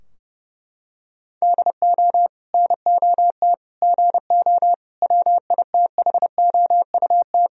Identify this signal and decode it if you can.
Morse code, heard as a keyed tone, about 30 wpm, 700 Hz. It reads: DO NOT GO WITHOUT
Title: Sentences from Top 1000 Words - Review Part 3 - 30wpm